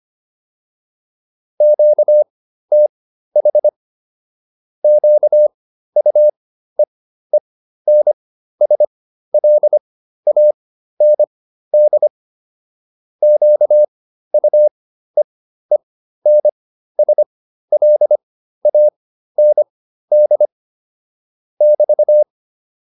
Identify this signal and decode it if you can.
Morse code, heard as a keyed tone, about 25 wpm, spaced as for 13 wpm, 600 Hz. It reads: QTH QUEENSLAND QUEENSLAND <BT>
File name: QSO 407 13